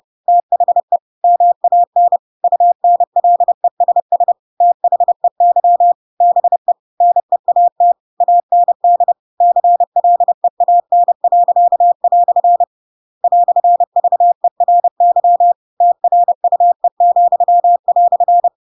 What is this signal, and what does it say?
Morse code, heard as a keyed tone, about 30 wpm, 700 Hz. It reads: THE MAN UNLESS THEY BE NEAT AND CLEAN." "VERY TRUE,"
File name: Pinocchio / Pinocchio ch8 30 30 ik1hge